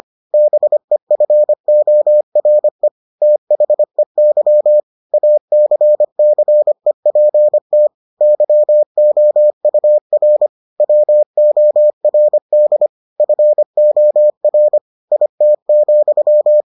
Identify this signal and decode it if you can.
Morse code, heard as a keyed tone, about 25 wpm, 600 Hz. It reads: BEFORE THEY ACCEPT YOUR WORD FOR IT,